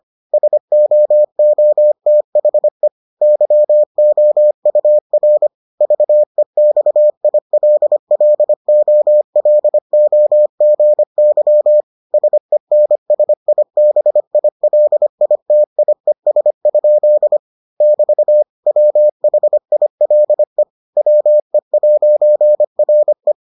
Morse code, heard as a keyed tone, about 25 wpm, 600 Hz. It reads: SOOTHE YOUR VEXILLOLOGY SENSIBILITIES? <BT> WHILE WE'RE